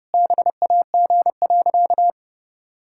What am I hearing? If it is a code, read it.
Morse code, heard as a keyed tone, about 30 wpm, 700 Hz. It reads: BAG.
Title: Pangram Challenge v2 - 30wpm